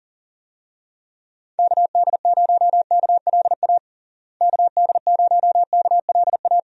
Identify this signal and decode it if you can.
Morse code, heard as a keyed tone, about 40 wpm, 700 Hz. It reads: KD0KLA KD0KLA
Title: USA Call Signs - 6 Characters - Repeated 3x - 40wpm